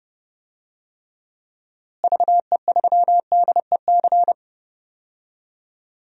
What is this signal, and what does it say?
Morse code, heard as a keyed tone, about 30 wpm, 700 Hz. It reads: VE3DEC
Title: DXCC-100 Call Signs - 6 Characters - Repeated 3x - 30wpm